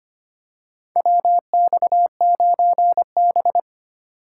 Morse code, heard as a keyed tone, about 25 wpm, 700 Hz. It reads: WX9B
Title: USA Call Signs - 2x Character Spacing - 25wpm